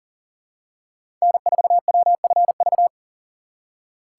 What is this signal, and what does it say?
Morse code, heard as a keyed tone, about 40 wpm, 700 Hz. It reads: N4WFV
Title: USA Call Signs - 4x Character Spacing - 40wpm